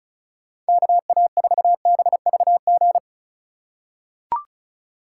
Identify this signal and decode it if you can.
Morse code, heard as a keyed tone, about 34 wpm, 700 Hz. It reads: KA4BVG
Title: K1USN SST QSO Practice - 35wpm